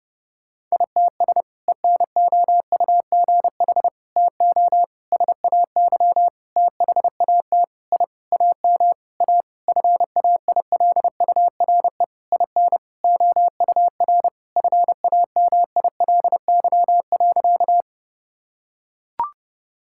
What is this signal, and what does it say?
Morse code, heard as a keyed tone, about 30 wpm, 700 Hz. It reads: ITS ENOUGH TO SAY THAT I AM A FAILURE IN OUR FAMILY.
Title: Sentences from Top 1000 Words - Review Part 6 - 30wpm